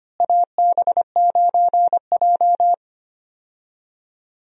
Morse code, heard as a keyed tone, about 25 wpm, 700 Hz. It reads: AB9J
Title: USA Call Signs - 4 Characters - Repeated 2x - 25wpm